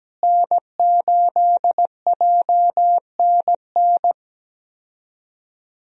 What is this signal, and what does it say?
Morse code, heard as a keyed tone, about 17 wpm, 700 Hz. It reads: N8JNN
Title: USA Call Signs - 5 Characters - Repeated 3x - 17wpm